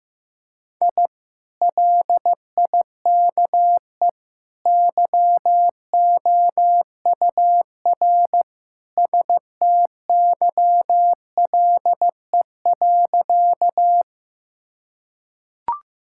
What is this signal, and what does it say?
Morse code, heard as a keyed tone, about 15 wpm, 700 Hz. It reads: I LIKE YOUR STYLE.
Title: Sentences from Top 1000 Words - Review Part 5 - 15wpm